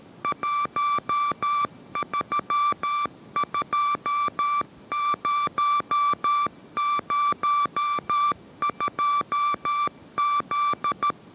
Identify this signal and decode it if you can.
Morse code, heard as a keyed tone, about 13 wpm, 1170 Hz. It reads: 132002Z